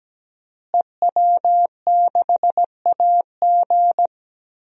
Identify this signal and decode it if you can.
Morse code, heard as a keyed tone, about 17 wpm, 700 Hz. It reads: EW6AG
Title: DXCC-100 Call Signs - 5 Characters - Repeated 3x - 17wpm